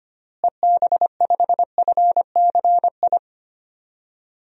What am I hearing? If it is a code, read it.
Morse code, heard as a keyed tone, about 25 wpm, 700 Hz. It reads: EB5FCI
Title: DXCC-100 Call Signs - 6 Characters - Repeated 3x - 25wpm